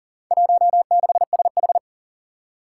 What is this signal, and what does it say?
Morse code, heard as a keyed tone, about 40 wpm, 700 Hz. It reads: 16SH
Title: Four Challenging Letter-Number Combinations 40wpm